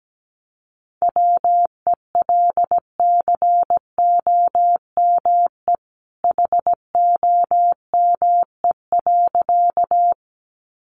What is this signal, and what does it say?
Morse code, heard as a keyed tone, about 17 wpm, 700 Hz. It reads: WELCOME HOME.
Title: A Sound Of Thunder 17wpm